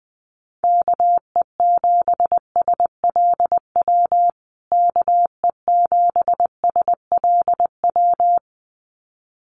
Morse code, heard as a keyed tone, about 19 wpm, 700 Hz. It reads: KE7SLW KE7SLW
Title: USA Call Signs Repeated 2x 20wpm